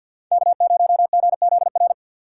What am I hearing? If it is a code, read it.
Morse code, heard as a keyed tone, about 53 wpm, 700 Hz. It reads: K0GZR
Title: USA Call Signs - 5 Characters - 4x Character Spacing - 50wpm